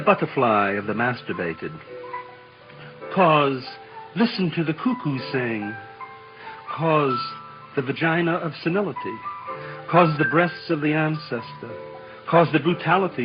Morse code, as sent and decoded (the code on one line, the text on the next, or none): none